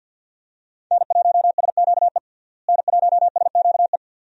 Morse code, heard as a keyed tone, about 47 wpm, 700 Hz. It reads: N1SQE N1SQE
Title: USA Call Signs - 5 Characters - Repeated 2x - 50wpm